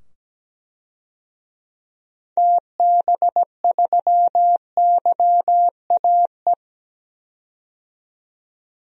Morse code, heard as a keyed tone, about 17 wpm, 700 Hz. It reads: TB3YAE
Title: DXCC-100 Call Signs - 6 Characters - 17wpm